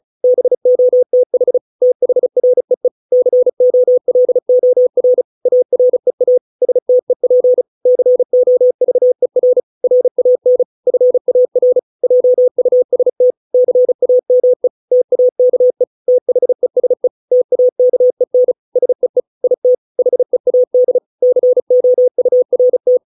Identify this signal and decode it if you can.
Morse code, heard as a keyed tone, about 34 wpm, 500 Hz. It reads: BOTH THREE COLOR AREA STEP COVER RAN FAR JUST CAME TAKE THESE TAKEN SEE IT HEAD COURT